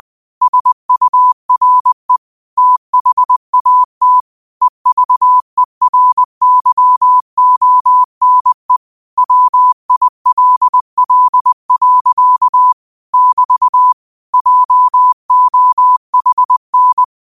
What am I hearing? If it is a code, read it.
Morse code, heard as a keyed tone, about 20 wpm, 1000 Hz. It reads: SURE THAT EVERYONE WILL. <BT> JOHN